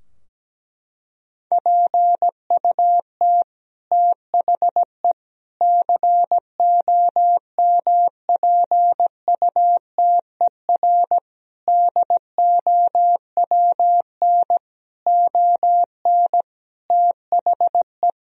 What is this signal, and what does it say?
Morse code, heard as a keyed tone, about 17 wpm, 700 Hz. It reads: PUT THE COMPUTER DOWN ON THE
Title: Sentences from Top 1000 Words - Review Part 3 - 17wpm